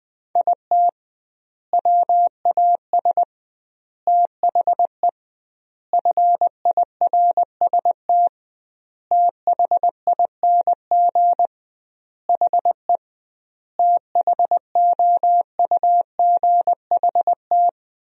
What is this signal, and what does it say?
Morse code, heard as a keyed tone, about 20 wpm, 700 Hz. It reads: IT WAS THE FIRST THING HE THOUGHT
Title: 2x Word-Spacing - Sentences from Top 300 Words - 20wpm